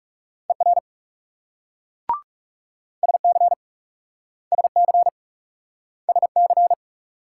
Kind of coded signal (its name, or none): Morse code